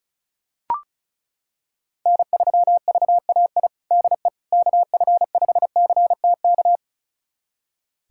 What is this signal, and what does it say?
Morse code, heard as a keyed tone, about 34 wpm, 700 Hz. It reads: N3VAI DE KF5CTK
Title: USA Callsign DE Callsign 35wpm